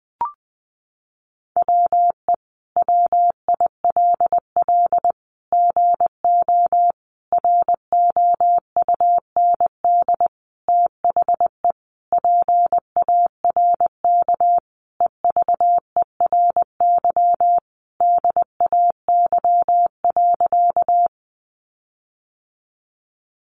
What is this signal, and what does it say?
Morse code, heard as a keyed tone, about 20 wpm, 700 Hz. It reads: WE WILL GO ROUND THE PARK EVERY DAY.
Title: Sentences from Top 1300 Words 20wpm